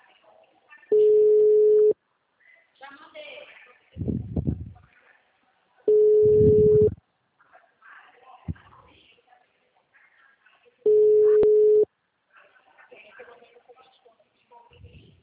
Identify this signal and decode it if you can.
Ringback tone (Europe, 425 Hz), with speech.
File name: out- -2049- - - .79555